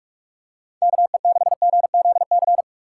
Morse code, heard as a keyed tone, about 46 wpm, 700 Hz. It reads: KE6GZC
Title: USA Call Signs - 6 Characters - 2x Character Spacing - 45wpm